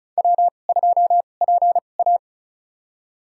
Morse code, heard as a keyed tone, about 36 wpm, spaced as for 25 wpm, 700 Hz. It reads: W2PA
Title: USA Call Signs - 4 Characters - 2x Character Spacing - 35wpm